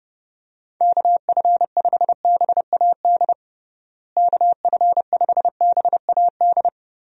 Morse code, heard as a keyed tone, about 32 wpm, 700 Hz. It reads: KF5BAD KF5BAD
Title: USA Call Signs - 6 Characters - Repeated 2x - 30wpm